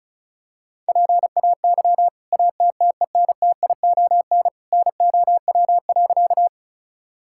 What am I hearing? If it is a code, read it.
Morse code, heard as a keyed tone, about 36 wpm, 700 Hz. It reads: PAY ATTENTION NOW.
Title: Sentences from Top 1000 Words - Review Part 6 - 35wpm